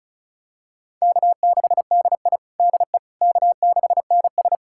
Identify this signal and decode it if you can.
Morse code, heard as a keyed tone, about 36 wpm, 700 Hz. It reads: K6DI DE K6NS
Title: USA Callsign DE Callsign - 4 Characters - 35wpm